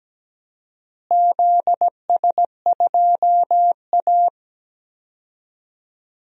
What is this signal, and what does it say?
Morse code, heard as a keyed tone, about 17 wpm, 700 Hz. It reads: ZS2A
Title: DXCC-100 Call Signs - 4 Characters - Repeated 2x - 17wpm